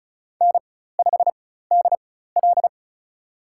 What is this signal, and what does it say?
Morse code, heard as a keyed tone, about 34 wpm, spaced as for 16 wpm, 700 Hz. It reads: N5DL
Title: USA Call Signs - 4x Character Spacing - 35wpm